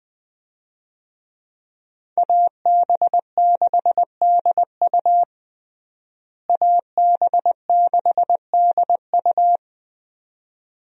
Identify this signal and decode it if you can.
Morse code, heard as a keyed tone, about 20 wpm, 700 Hz. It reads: AB6DU AB6DU
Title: USA Call Signs - 5 Characters - Repeated 2x - 20wpm